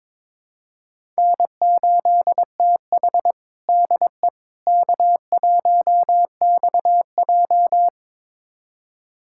Morse code, heard as a keyed tone, about 22 wpm, 700 Hz. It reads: N8TH DE K1XJ
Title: USA Callsign DE Callsign - 4 Characters - 22wpm